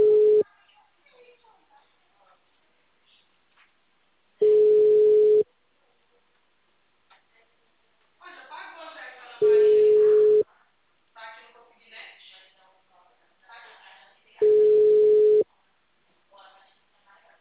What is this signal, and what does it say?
Ringback tone (Europe, 425 Hz), with speech.